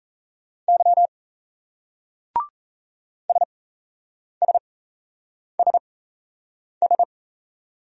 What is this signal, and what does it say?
Morse code, heard as a keyed tone, about 34 wpm, spaced as for 8 wpm, 700 Hz. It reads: Y SSSS
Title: Speed-Racing Single Letters 28wpm